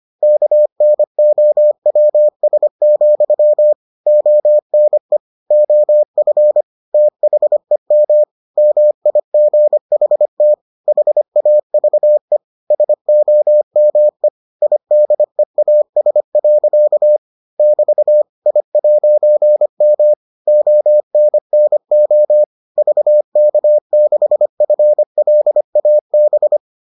Morse code, heard as a keyed tone, about 25 wpm, 600 Hz. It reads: KNOWS, ONE OF THEM MIGHT HAVE SOME IDEAS. <BT> I'M ONNO VK6FLAB